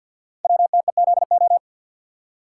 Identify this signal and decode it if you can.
Morse code, heard as a keyed tone, about 50 wpm, 700 Hz. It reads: WTEZO
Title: Code Groups - 5 Letters - 50wpm